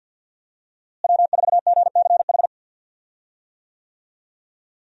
Morse code, heard as a keyed tone, about 52 wpm, 700 Hz. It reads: W4GCH
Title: USA Call Signs - 5 Characters - Repeated 3x - 50wpm